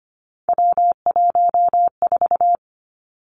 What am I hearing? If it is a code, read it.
Morse code, heard as a keyed tone, about 25 wpm, 700 Hz. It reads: W14
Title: Three Challenging Letter-Number Combinations 25wpm